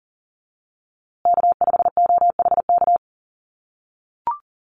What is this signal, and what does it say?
Morse code, heard as a keyed tone, about 39 wpm, 700 Hz. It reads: K5OHK
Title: USA Call Signs Repeated 2x 40wpm